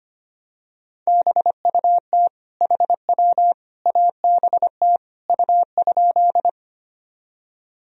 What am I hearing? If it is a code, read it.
Morse code, heard as a keyed tone, about 25 wpm, 700 Hz. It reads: BUT HW ABT U?
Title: Sentences from Top 100 Words and Abbreviations 25wpm